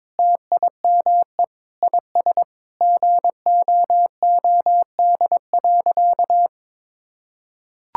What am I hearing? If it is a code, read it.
Morse code, heard as a keyed tone, about 22 wpm, 700 Hz. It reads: TIME IS GOOD.
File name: practice / 22wpm / Sentences with 3-Word Phrases - 22wpm